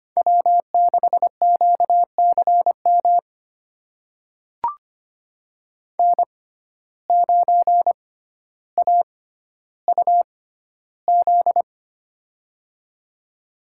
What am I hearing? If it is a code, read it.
Morse code, heard as a keyed tone, about 25 wpm, spaced as for 13 wpm, 700 Hz. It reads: W6QCM N9AUZ
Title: USA Call Signs - 5 Characters - 6x Character Spacing - 25wpm